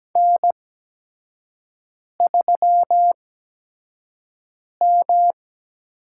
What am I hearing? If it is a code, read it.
Morse code, heard as a keyed tone, about 17 wpm, spaced as for 5 wpm, 700 Hz. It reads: N3M
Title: USA Call Signs - 5 Characters - 8x Character Spacing - 17wpm